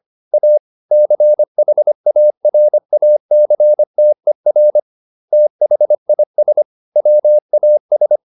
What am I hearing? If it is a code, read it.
Morse code, heard as a keyed tone, about 25 wpm, 600 Hz. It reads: A CHARACTER THIS WAS
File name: SS315 25